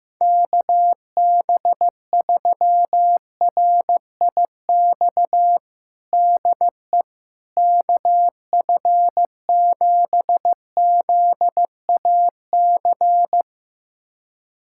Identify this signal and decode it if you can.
Morse code, heard as a keyed tone, about 15 wpm, 700 Hz. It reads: KB3RIX DE KF7ZAC
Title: USA Callsign DE Callsign - 6 Characters - 15wpm